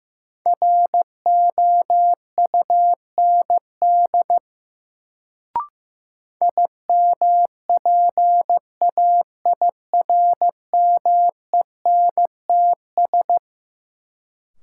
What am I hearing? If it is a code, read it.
Morse code, heard as a keyed tone, about 15 wpm, 700 Hz. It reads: ROUND IMPAIRMENTS
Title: Sets of 5 Words Encore - New Word Drill - 15wpm